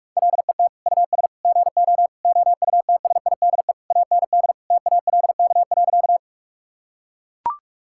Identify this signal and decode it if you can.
Morse code, heard as a keyed tone, about 42 wpm, 700 Hz. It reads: LET US GO OUTSIDE AND TALK.
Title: Sentences from Top 1000 Words - Review Part 6 - 45wpm